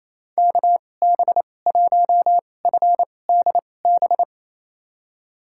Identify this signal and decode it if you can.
Morse code, heard as a keyed tone, about 28 wpm, spaced as for 20 wpm, 700 Hz. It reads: KB1FDB
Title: USA Call Signs - 2x Character Spacing - 28wpm